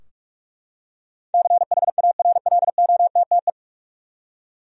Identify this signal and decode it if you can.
Morse code, heard as a keyed tone, about 41 wpm, 700 Hz. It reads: CHARLOTTE
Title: USA Common Names - 45wpm